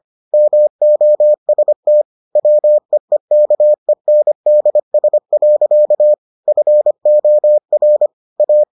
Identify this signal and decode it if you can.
Morse code, heard as a keyed tone, about 25 wpm, 600 Hz. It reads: MOST WEEKENDS. FOR A